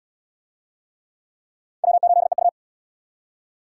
Morse code, heard as a keyed tone, about 69 wpm, 700 Hz. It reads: W8EU